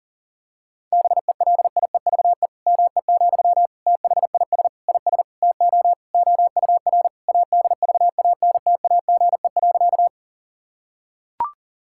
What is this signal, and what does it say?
Morse code, heard as a keyed tone, about 40 wpm, 700 Hz. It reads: BELIEVE ME, THIS IS TO OUR ADVANTAGE.
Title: Sentences from Top 1000 Words - Review Part 6 - 40wpm